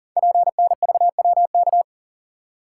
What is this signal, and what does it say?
Morse code, heard as a keyed tone, about 38 wpm, 700 Hz. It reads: PNVWK
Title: Code Groups - 5 Letters - 40wpm